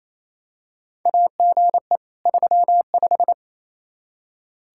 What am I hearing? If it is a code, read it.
Morse code, heard as a keyed tone, about 28 wpm, 700 Hz. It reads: AGE 35
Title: Practice QSO - Exchange 3 - More Info - 28wpm